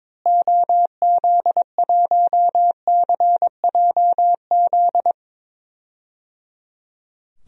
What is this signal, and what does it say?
Morse code, heard as a keyed tone, about 22 wpm, 700 Hz. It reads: OZ1CJZ
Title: DXCC-100 Call Signs - 6 Characters - Repeated 3x - 22wpm